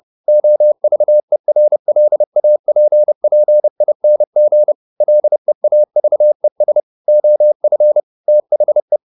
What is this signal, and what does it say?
Morse code, heard as a keyed tone, about 30 wpm, 600 Hz. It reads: OVERLAPPING LEAVES OF THE
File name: the old man and the sea / the old man and the sea 1 30wpm